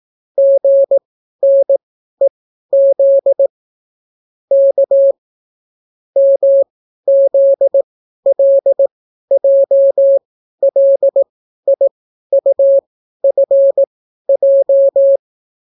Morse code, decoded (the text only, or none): GNEZ K MZLJLIUFJ